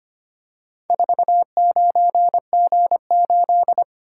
Morse code, heard as a keyed tone, about 25 wpm, 700 Hz. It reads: 49G8
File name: practice / 25wpm / Four Letter-Number Combinations - 25wpm